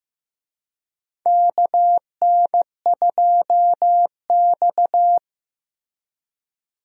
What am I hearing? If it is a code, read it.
Morse code, heard as a keyed tone, about 15 wpm, 700 Hz. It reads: KN2X
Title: USA Call Signs - 4 Characters - Repeated 3x - 15wpm